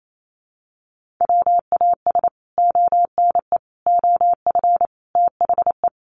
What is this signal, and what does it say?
Morse code, heard as a keyed tone, about 28 wpm, 700 Hz. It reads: WAS ONE OF THE
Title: Sets of 5 Words 28wpm